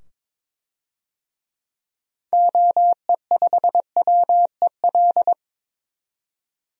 Morse code, heard as a keyed tone, about 22 wpm, 700 Hz. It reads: OE5WEL
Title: DXCC-100 Call Signs - 6 Characters - 22wpm